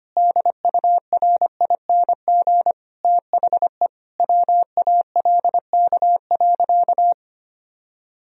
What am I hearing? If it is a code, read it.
Morse code, heard as a keyed tone, about 25 wpm, 700 Hz. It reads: DURING THE WALK.
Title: Sentences from Top 1000 Words - Review Part 2 - 25wpm